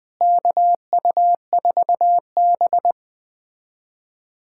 Morse code, heard as a keyed tone, about 20 wpm, 700 Hz. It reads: KU4B
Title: USA Callsign DE Callsign 20wpm